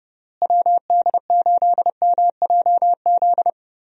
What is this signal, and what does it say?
Morse code, heard as a keyed tone, about 30 wpm, 700 Hz. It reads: WD8MJZ
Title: USA Call Signs - 6 Characters - Repeated 3x - 30wpm